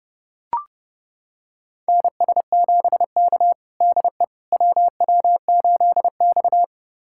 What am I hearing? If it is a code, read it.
Morse code, heard as a keyed tone, about 32 wpm, 700 Hz. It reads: NS7K DE WW8X
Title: USA Callsign DE Callsign - 4 Characters - 30wpm